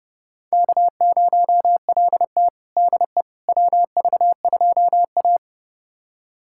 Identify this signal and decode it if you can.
Morse code, heard as a keyed tone, about 32 wpm, 700 Hz. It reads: K0LT DE WV2A
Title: USA Callsign DE Callsign - 4 Characters - 30wpm